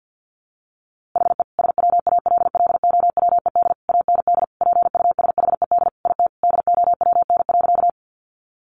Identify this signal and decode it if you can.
Morse code, heard as a keyed tone, about 50 wpm, 700 Hz. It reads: HE SWALLOWED AND PUSHED IT DOWN.
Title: A Sound Of Thunder 50wpm